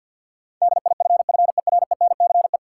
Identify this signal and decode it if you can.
Morse code, heard as a keyed tone, about 53 wpm, 700 Hz. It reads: DIFFERENCE
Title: Sentences from Top 800 Words - New Word Drill - 50wpm